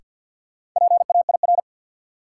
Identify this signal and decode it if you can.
Morse code, heard as a keyed tone, about 46 wpm, 700 Hz. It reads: PAIR
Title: Sets of 3 Words - New Word Drill - 50wpm